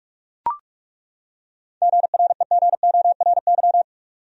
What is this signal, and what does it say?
Morse code, heard as a keyed tone, about 43 wpm, 700 Hz. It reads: GREGORY
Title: USA Common Names - 45wpm